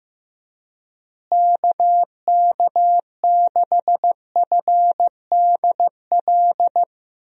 Morse code, heard as a keyed tone, about 16 wpm, 700 Hz. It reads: KK6FDL
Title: USA Call Signs - 6 Characters - Repeated 3x - 15wpm